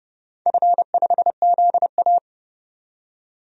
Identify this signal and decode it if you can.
Morse code, heard as a keyed tone, about 30 wpm, 700 Hz. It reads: F5ZA
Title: DXCC-100 Call Signs - 4 Characters - 30wpm